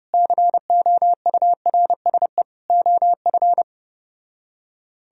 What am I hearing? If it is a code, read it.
Morse code, heard as a keyed tone, about 33 wpm, 700 Hz. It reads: COURSE OF